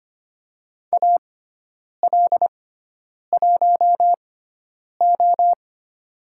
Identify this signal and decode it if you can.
Morse code, heard as a keyed tone, about 25 wpm, spaced as for 9 wpm, 700 Hz. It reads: AL1O
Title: USA Call Signs - 4 Characters - 6x Character Spacing - 25wpm